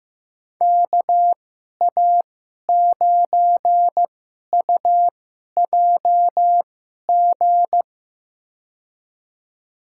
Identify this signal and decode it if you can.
Morse code, heard as a keyed tone, about 15 wpm, spaced as for 11 wpm, 700 Hz. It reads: KA9UJG
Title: USA Call Signs - 6 Characters - 2x Character Spacing - 15wpm